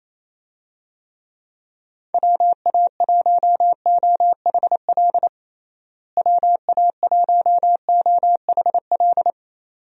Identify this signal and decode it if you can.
Morse code, heard as a keyed tone, about 28 wpm, 700 Hz. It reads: WA1OHL WA1OHL
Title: USA Call Signs - 6 Characters - Repeated 2x - 28wpm